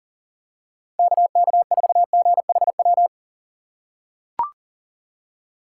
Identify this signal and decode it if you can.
Morse code, heard as a keyed tone, about 40 wpm, 700 Hz. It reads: KK4GHW
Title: USA Call Signs - 4x Character Spacing - 40wpm